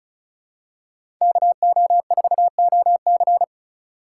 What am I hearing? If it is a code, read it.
Morse code, heard as a keyed tone, about 36 wpm, 700 Hz. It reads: KO4OC
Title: USA Call Signs - 5 Characters - 2x Character Spacing - 35wpm